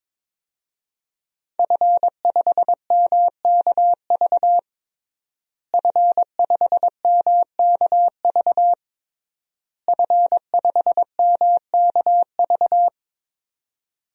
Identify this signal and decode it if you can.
Morse code, heard as a keyed tone, about 22 wpm, 700 Hz. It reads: F5MKV F5MKV F5MKV
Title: DXCC-100 Call Signs - 5 Characters - Repeated 3x - 22wpm